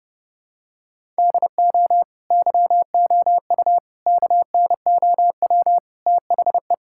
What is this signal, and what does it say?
Morse code, heard as a keyed tone, about 32 wpm, 700 Hz. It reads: DO YOU KNOW THE
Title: Sentences from Top 1000 Words - Review Part 6 - 30wpm